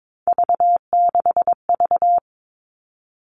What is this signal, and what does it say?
Morse code, heard as a keyed tone, about 22 wpm, 700 Hz. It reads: V6V
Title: All Cut Numbers - 3-Digit 22wpm